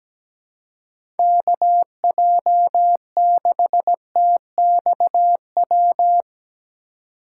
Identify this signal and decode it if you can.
Morse code, heard as a keyed tone, about 17 wpm, 700 Hz. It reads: KJ6TXW
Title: USA Call Signs - 6 Characters - 4x Character Spacing - 17wpm